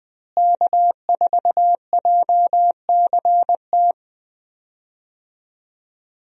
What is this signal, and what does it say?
Morse code, heard as a keyed tone, about 21 wpm, 700 Hz. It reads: K4JCT
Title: USA Call Signs - 5 Characters - Repeated 3x - 20wpm